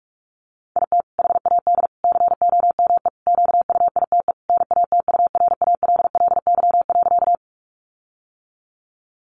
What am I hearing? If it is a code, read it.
Morse code, heard as a keyed tone, about 44 wpm, 700 Hz. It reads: IT HAD COME QUITE NATURALLY.
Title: Sentences from Top 1300 Words 45wpm